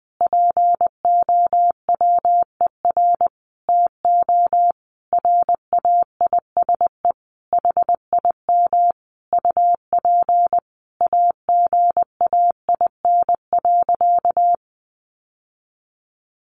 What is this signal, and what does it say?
Morse code, heard as a keyed tone, about 20 wpm, 700 Hz. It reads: POWER TO RAISE HIM UP AGAIN.
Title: Sentences from Top 1400 Words 20wpm